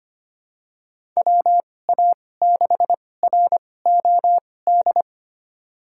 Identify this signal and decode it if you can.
Morse code, heard as a keyed tone, about 25 wpm, spaced as for 18 wpm, 700 Hz. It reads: WA6ROD
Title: USA Call Signs - 2x Character Spacing - 25wpm